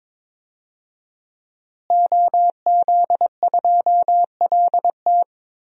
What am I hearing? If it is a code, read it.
Morse code, heard as a keyed tone, about 22 wpm, 700 Hz. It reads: OZ2LT
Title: DXCC-100 Call Signs - 5 Characters - Repeated 2x - 22wpm